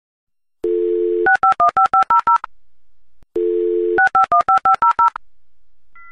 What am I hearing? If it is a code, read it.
DTMF, heard as telephone touch tones, tones of about 120 ms. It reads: 6515500 6515500